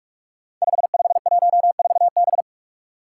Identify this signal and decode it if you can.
Morse code, heard as a keyed tone, about 42 wpm, 700 Hz. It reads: 5514B
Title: Five Challenging Letter-Number Combinations 45wpm